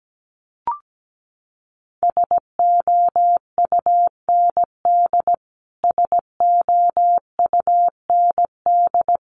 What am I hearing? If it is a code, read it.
Morse code, heard as a keyed tone, about 17 wpm, 700 Hz. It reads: SOUND SOUND
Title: Top 500 Words Repeated 17wpm